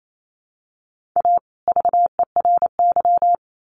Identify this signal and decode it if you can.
Morse code, heard as a keyed tone, about 28 wpm, 700 Hz. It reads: A VERY
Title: Sets of 2 Words 28wpm